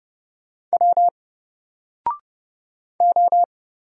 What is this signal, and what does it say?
Morse code, heard as a keyed tone, about 29 wpm, 700 Hz. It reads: W O
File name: Single-Letter-Number-30wpm